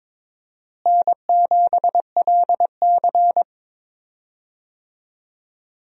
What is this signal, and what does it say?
Morse code, heard as a keyed tone, about 22 wpm, 700 Hz. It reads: N7LC
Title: USA Call Signs - 4 Characters - Repeated 2x - 22wpm